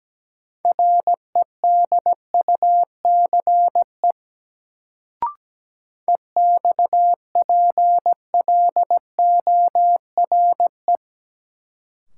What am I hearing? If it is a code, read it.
Morse code, heard as a keyed tone, about 17 wpm, 700 Hz. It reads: REDUCE EXPLORE
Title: Sentences from Top 1100 Words - New Word Drill - 17wpm